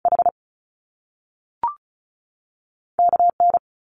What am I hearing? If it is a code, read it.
Morse code, heard as a keyed tone, about 33 wpm, 700 Hz. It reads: H KN
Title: Top Two Letter Combinations in Words 35wpm